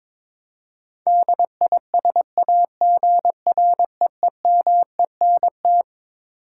Morse code, heard as a keyed tone, about 22 wpm, 700 Hz. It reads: DISAGREEMENT
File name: Twelve Letter Words - 22wpm